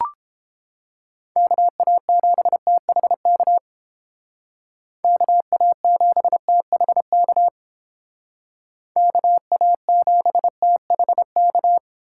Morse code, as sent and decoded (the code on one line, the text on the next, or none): -.- .- --... - .... -.- / -.- .- --... - .... -.- / -.- .- --... - .... -.-
KA7THK KA7THK KA7THK